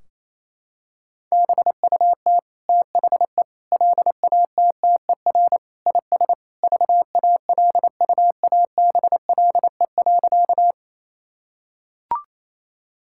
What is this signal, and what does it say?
Morse code, heard as a keyed tone, about 27 wpm, 700 Hz. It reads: BUT THE LATTER IS VALUABLE.
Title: Sentences from Top 1600 Words 28wpm